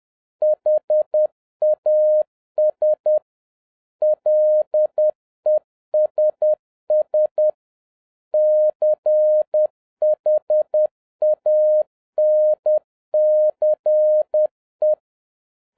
Morse code, heard as a keyed tone, about 10 wpm, 610 Hz. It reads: HAS LESS CHANCE
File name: 2011-07-mp3 / n41 EXCITING QRP CW OPERATION 10wpm